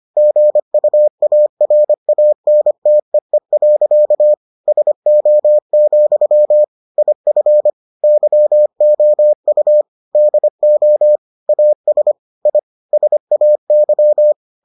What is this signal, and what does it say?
Morse code, heard as a keyed tone, about 25 wpm, 600 Hz. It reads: GUARANTEE. SO, IF YOU DO AS I SAY